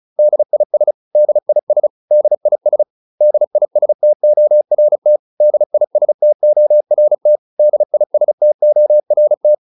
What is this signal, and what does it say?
Morse code, heard as a keyed tone, about 34 wpm, 600 Hz. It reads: DIS DIS DIS DISTORT DISTORT DISTORT